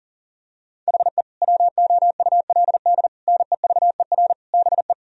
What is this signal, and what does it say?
Morse code, heard as a keyed tone, about 40 wpm, 700 Hz. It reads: HE WOULD NEVER BE
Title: Sentences from Top 1000 Words - Review Part 3 - 40wpm